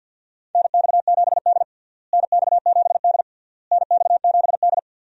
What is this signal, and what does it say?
Morse code, heard as a keyed tone, about 51 wpm, 700 Hz. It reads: NX7D NX7D NX7D
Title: USA Call Signs - 4 Characters - Repeated 3x - 50wpm